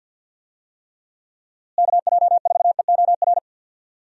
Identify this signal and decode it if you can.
Morse code, heard as a keyed tone, about 52 wpm, 700 Hz. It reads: KJ4EOR
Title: USA Call Signs - 6 Characters - Repeated 2x - 50wpm